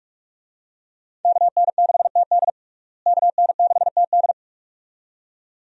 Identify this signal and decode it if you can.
Morse code, heard as a keyed tone, about 46 wpm, 700 Hz. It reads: KN6TD KN6TD
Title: USA Call Signs - 5 Characters - Repeated 2x - 45wpm